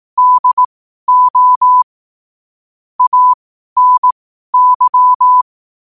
Morse code, heard as a keyed tone, about 18 wpm, spaced as for 12 wpm, 1000 Hz. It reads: DO ANY
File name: qotdfiles / 12wpm